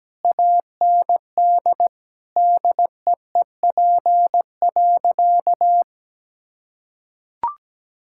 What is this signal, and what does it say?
Morse code, heard as a keyed tone, about 17 wpm, 700 Hz. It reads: AND DEEP.
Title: Sentences from Top 1000 Words - Review Part 6 - 17wpm